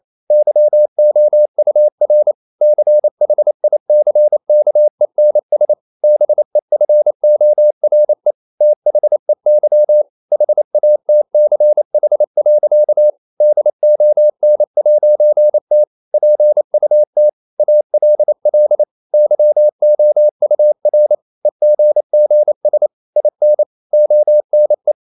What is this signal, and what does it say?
Morse code, heard as a keyed tone, about 28 wpm, 600 Hz. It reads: YOUR CHICKENS BEFORE THEY HATCH. DON'T PUT ALL YOUR EGGS IN ONE
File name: CW / Practice / EnglishSayings28